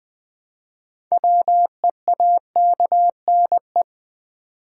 Morse code, heard as a keyed tone, about 20 wpm, 700 Hz. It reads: WEAKNE
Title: Build A Word - 2 Letters At A Time - 8-Letter Words - 20wpm